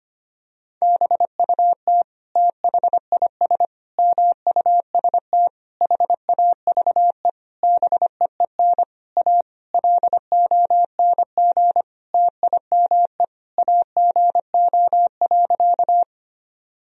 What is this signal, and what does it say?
Morse code, heard as a keyed tone, about 25 wpm, 700 Hz. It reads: BUT THIS MUST HAVE BEEN A LONG TIME AGO.
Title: Sentences from Top 1000 Words - Review Part 3 - 25wpm